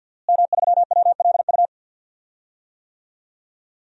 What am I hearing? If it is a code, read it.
Morse code, heard as a keyed tone, about 52 wpm, 700 Hz. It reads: M3WLU
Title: DXCC-100 Call Signs - 5 Characters - Repeated 3x - 50wpm